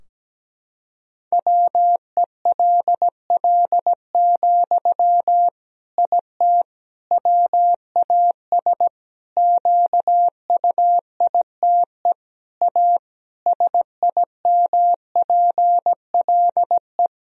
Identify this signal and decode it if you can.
Morse code, heard as a keyed tone, about 17 wpm, 700 Hz. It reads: WELL, IT WAS QUITE A SIMPLE
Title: Sentences from Top 1000 Words - Review Part 3 - 17wpm